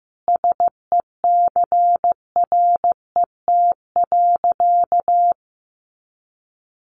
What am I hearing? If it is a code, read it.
Morse code, heard as a keyed tone, about 15 wpm, 700 Hz. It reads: SECRET.
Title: Sentences from Top 1300 Words 15wpm